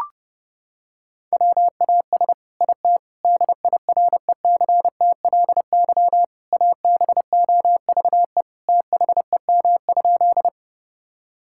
Morse code, heard as a keyed tone, about 30 wpm, 700 Hz. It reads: WAS IT DIRECTLY ABOVE THEM?
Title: Sentences from Top 1000 Words - Review Part 5 - 30wpm